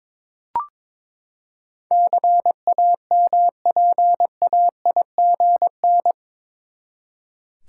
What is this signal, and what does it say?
Morse code, heard as a keyed tone, about 21 wpm, 700 Hz. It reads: CAMPAIGN
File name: practice / 22wpm / Eight Letter Words - 22wpm